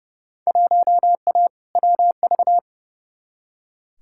Morse code, heard as a keyed tone, about 30 wpm, 700 Hz. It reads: 1A WV
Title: Field Day Exchanges Encore - 30wpm